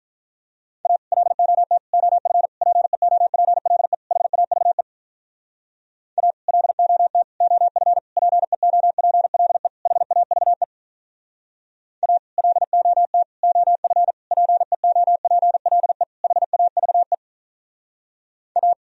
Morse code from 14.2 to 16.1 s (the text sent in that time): PEOPLE